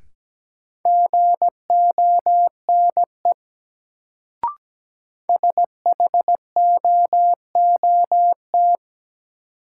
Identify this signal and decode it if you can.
Morse code, heard as a keyed tone, about 17 wpm, 700 Hz. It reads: GONE SHOOT
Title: Sentences from Top 1200 Words - New Word Drill Alt - 17wpm